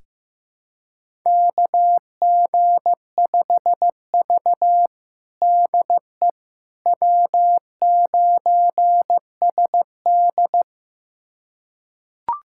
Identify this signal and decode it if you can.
Morse code, heard as a keyed tone, about 15 wpm, 700 Hz. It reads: KG5V DE W9SD
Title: USA Callsign DE Callsign - 4 Characters - 15wpm